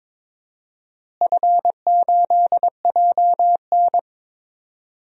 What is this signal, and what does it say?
Morse code, heard as a keyed tone, about 22 wpm, 700 Hz. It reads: F8JN
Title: DXCC-100 Call Signs - 4 Characters - Repeated 3x - 22wpm